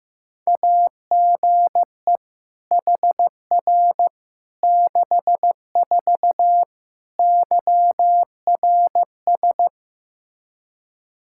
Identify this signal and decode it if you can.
Morse code, heard as a keyed tone, about 15 wpm, 700 Hz. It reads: AGE HR 64 YRS
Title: Practice QSO - Exchange 3 - More Info - 15wpm